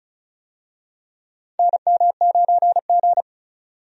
Morse code, heard as a keyed tone, about 34 wpm, 700 Hz. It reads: NM9G
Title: USA Call Signs - 4 Characters - 35wpm